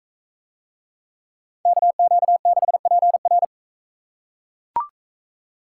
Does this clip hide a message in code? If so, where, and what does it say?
Yes: Morse code from 1.7 to 3.4 s, heard as a keyed tone, about 42 wpm, 700 Hz. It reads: KQ6PR